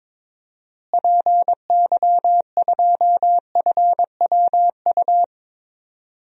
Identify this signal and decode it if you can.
Morse code, heard as a keyed tone, about 22 wpm, 700 Hz. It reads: PY2FWU
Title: DXCC-100 Call Signs - 6 Characters - Repeated 2x - 22wpm